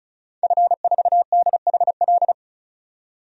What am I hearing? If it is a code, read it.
Morse code, heard as a keyed tone, about 34 wpm, 700 Hz. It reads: F4DHL
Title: DXCC-100 Call Signs - 5 Characters - 35wpm